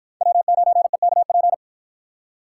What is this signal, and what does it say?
Morse code, heard as a keyed tone, about 49 wpm, 700 Hz. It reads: W9EKP